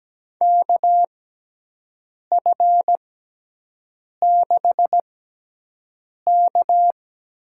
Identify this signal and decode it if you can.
Morse code, heard as a keyed tone, about 17 wpm, spaced as for 6 wpm, 700 Hz. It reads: KF6K
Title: USA Call Signs - 6 Characters - 6x Character Spacing - 17wpm